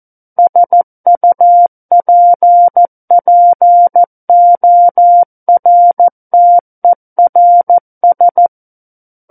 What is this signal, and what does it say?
Morse code, heard as a keyed tone, about 14 wpm, 700 Hz. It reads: SUPPORTERS
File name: Book One / 05-Chapter 5 - 14 WPM